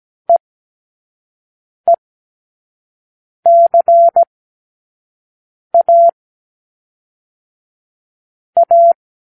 Morse code, heard as a keyed tone, about 17 wpm, 680 Hz. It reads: E E C A A
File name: k7qo / Lesson9(A-F)